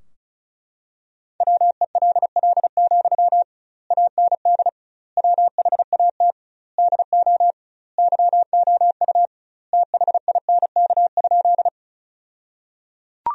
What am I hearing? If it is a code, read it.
Morse code, heard as a keyed tone, about 36 wpm, 700 Hz. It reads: WELL, AND WHAT DO YOU THINK?
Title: 2x Word-Spacing - Sentences from Top 200 Words - 35wpm